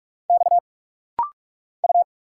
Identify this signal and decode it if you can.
Morse code, heard as a keyed tone, about 44 wpm, 700 Hz. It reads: X U
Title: Single Letter-Number - Rapid-Fire 45wpm